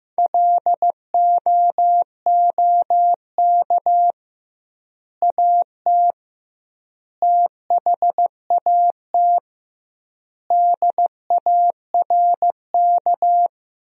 Morse code, heard as a keyed tone, about 16 wpm, 700 Hz. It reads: LOOK AT THAT DARK
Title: 2x Word-Spacing - Sentences from Top 500 Words - 15wpm